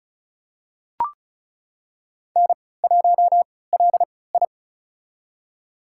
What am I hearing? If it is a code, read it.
Morse code, heard as a keyed tone, about 36 wpm, spaced as for 20 wpm, 700 Hz. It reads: N1LI
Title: USA Call Signs - 4 Characters - 3x Character Spacing - 35wpm